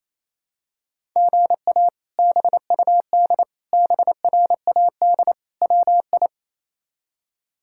Morse code, heard as a keyed tone, about 27 wpm, 700 Hz. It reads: GA BUD BRAD WI
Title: K1USN SST QSO Practice - 28wpm